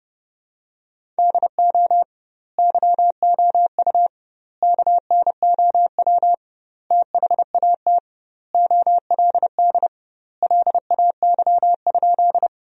Morse code, heard as a keyed tone, about 30 wpm, 700 Hz. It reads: DO YOU KNOW THAT OLD LAY?
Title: 2x Word-Spacing - Sentences from Top 400 Words - 30wpm